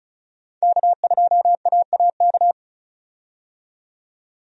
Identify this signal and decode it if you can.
Morse code, heard as a keyed tone, about 36 wpm, 700 Hz. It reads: K2AAK
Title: USA Call Signs - 5 Characters - 35wpm